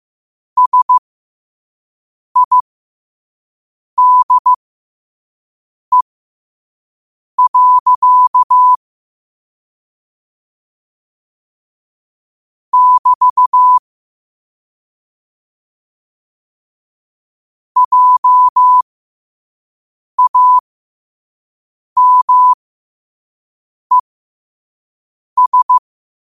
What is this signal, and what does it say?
Morse code, heard as a keyed tone, about 15 wpm, spaced as for 5 wpm, 1000 Hz. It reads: SIDE. <BT> JAMES